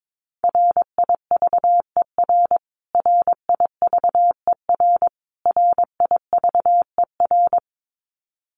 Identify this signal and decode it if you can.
Morse code, heard as a keyed tone, about 22 wpm, 700 Hz. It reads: RIVER RIVER RIVER
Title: Top 500 Words Repeated 22wpm